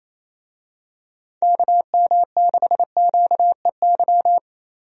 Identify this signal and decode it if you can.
Morse code, heard as a keyed tone, about 28 wpm, 700 Hz. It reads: KM6QEY
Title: USA Call Signs - 6 Characters - Repeated 2x - 28wpm